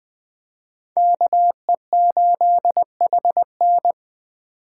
Morse code, heard as a keyed tone, about 20 wpm, 700 Hz. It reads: KE8HN
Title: USA Call Signs - 5 Characters - 4x Character Spacing - 20wpm